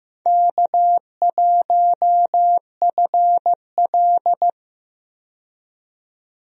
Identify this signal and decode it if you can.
Morse code, heard as a keyed tone, about 15 wpm, 700 Hz. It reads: K1FL
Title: USA Call Signs - 4 Characters - 15wpm